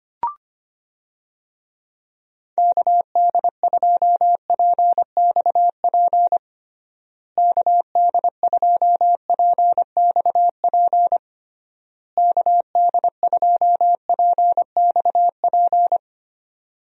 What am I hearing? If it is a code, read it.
Morse code, heard as a keyed tone, about 25 wpm, 700 Hz. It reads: KD2PXP KD2PXP KD2PXP
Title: USA Call Signs - 6 Characters - Repeated 3x - 25wpm